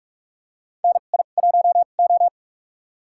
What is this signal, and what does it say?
Morse code, heard as a keyed tone, about 46 wpm, spaced as for 33 wpm, 700 Hz. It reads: NI1O
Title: USA Call Signs - 4 Characters - 2x Character Spacing - 45wpm